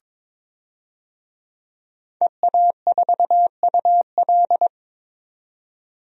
Morse code, heard as a keyed tone, about 22 wpm, 700 Hz. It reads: EA4UL
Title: DXCC-100 Call Signs - 5 Characters - 22wpm